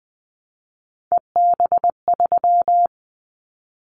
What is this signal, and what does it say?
Morse code, heard as a keyed tone, about 20 wpm, 700 Hz. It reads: EB3
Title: Callsign Prefixes for Top 25 DXCC - 20wpm